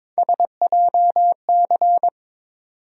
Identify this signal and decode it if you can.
Morse code, heard as a keyed tone, about 22 wpm, 700 Hz. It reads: SJC
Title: USA Call Sign Postfixes - 3 Letters 22wpm